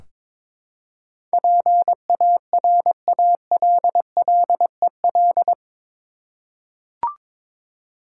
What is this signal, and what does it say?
Morse code, heard as a keyed tone, about 22 wpm, 700 Hz. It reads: PARALLEL
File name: practice / 22wpm / Eight Letter Words - 22wpm